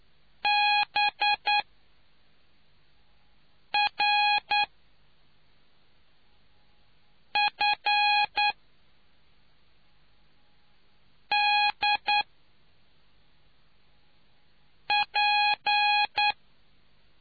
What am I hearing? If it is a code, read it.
Morse code, heard as a keyed tone, about 9 wpm, spaced as for 3 wpm, 830 Hz. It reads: BRFDP